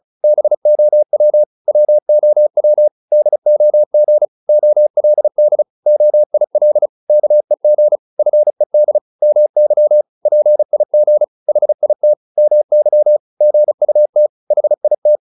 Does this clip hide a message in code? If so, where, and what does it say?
Yes: Morse code from 0.2 to 15.2 s, heard as a keyed tone, about 34 wpm, 600 Hz. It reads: BOW WOW DOG OLD OIL KEG FED MY PIG HIT MY GUT HIT